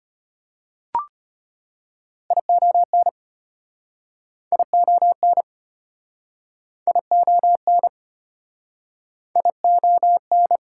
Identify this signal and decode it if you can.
Morse code, heard as a keyed tone, about 37 wpm, 700 Hz. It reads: ION ION ION ION